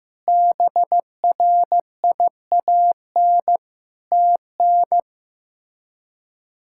Morse code, heard as a keyed tone, about 15 wpm, 700 Hz. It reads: BRIAN TN
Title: K1USN SST QSO Practice - 15wpm